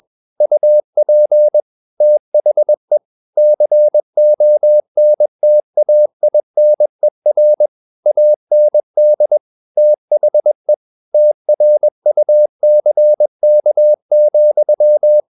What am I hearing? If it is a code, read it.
Morse code, heard as a keyed tone, about 21 wpm, 600 Hz. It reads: UP THE CONTAINER AND THE TRUCK,
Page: CW med Gnister